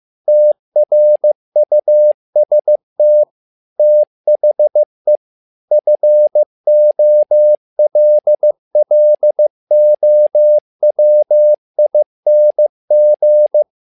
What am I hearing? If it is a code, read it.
Morse code, heard as a keyed tone, about 15 wpm, 600 Hz. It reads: TRUST THE FOLLOWING
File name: learning center / tower trouble / tower trouble 15wpm